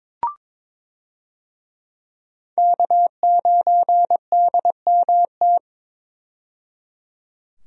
Morse code, heard as a keyed tone, about 22 wpm, 700 Hz. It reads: K9DMT
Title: USA Call Signs - 5 Characters - 22wpm